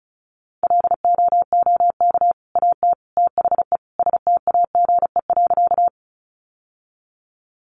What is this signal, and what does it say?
Morse code, heard as a keyed tone, about 34 wpm, 700 Hz. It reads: LOOK AT THE STAGE.
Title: Sentences from Top 1200 Words 35wpm